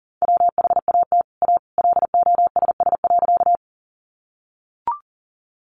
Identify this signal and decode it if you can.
Morse code, heard as a keyed tone, about 40 wpm, 700 Hz. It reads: WHAT A LOSS.
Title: Sentences from Top 1400 Words 40wpm